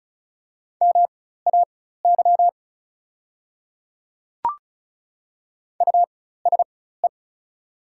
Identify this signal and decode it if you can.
Morse code, heard as a keyed tone, about 36 wpm, spaced as for 16 wpm, 690 Hz. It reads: MAY USE
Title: 4x Character Spacing - Top 300 Words - 35wpm